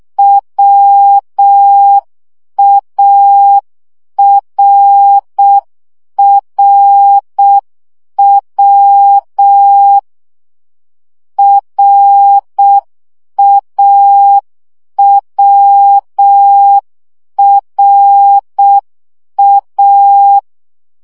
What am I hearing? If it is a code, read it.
Morse code, heard as a keyed tone, about 6 wpm, 800 Hz. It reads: WARRW RAWRA
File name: CW gyakorlas a r w 5 baud